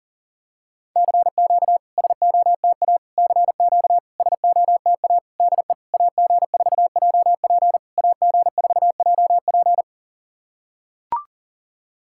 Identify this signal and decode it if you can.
Morse code, heard as a keyed tone, about 40 wpm, 690 Hz. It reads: CQ SOTA CQ SOTA DE AG4JP AG4JP
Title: SOTA QSO Practice 40wpm